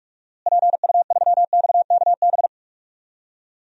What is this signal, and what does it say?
Morse code, heard as a keyed tone, about 45 wpm, 700 Hz. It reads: PU3XKB
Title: DXCC-100 Call Signs - 6 Characters - Repeated 2x - 45wpm